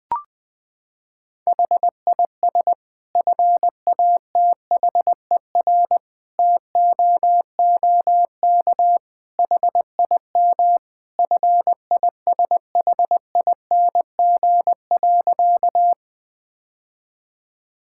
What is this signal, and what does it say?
Morse code, heard as a keyed tone, about 20 wpm, 700 Hz. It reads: HIS FATHER TOOK HIM FISHING.
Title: Sentences from Top 1000 Words - Review Part 5 - 20wpm